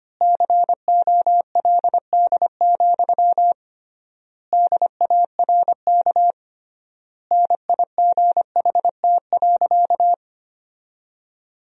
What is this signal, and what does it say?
Morse code, heard as a keyed tone, about 25 wpm, 700 Hz. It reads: COLD, DARK NIGHT.
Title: 3x Word-Spacing - Sentences from Top 500 Words - 25wpm